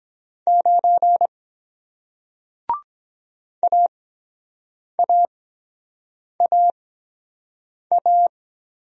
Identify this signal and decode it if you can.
Morse code, heard as a keyed tone, about 23 wpm, spaced as for 7 wpm, 700 Hz. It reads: 9 AAAA